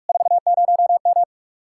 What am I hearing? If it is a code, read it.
Morse code, heard as a keyed tone, about 57 wpm, 700 Hz. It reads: 40M